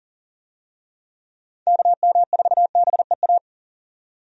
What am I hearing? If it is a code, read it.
Morse code, heard as a keyed tone, about 39 wpm, 700 Hz. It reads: KM4BEA
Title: USA Call Signs - 6 Characters - 40wpm